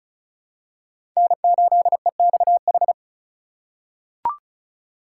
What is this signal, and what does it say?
Morse code, heard as a keyed tone, about 36 wpm, 690 Hz. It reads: N8EXH
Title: USA Call Signs - 3x Character Spacing - 35wpm